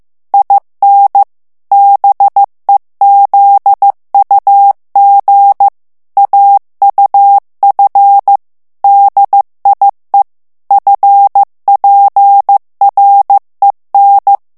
Morse code, heard as a keyed tone, about 15 wpm, 800 Hz. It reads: IN BEZUG AUF DIE FPREN
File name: morse mp3 / ALLEMAND texte clair 1 vitesse 14 mots minute